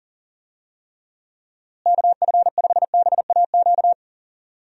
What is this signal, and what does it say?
Morse code, heard as a keyed tone, about 40 wpm, 700 Hz. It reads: KF5BAQ
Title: USA Call Signs - 6 Characters - Repeated 2x - 40wpm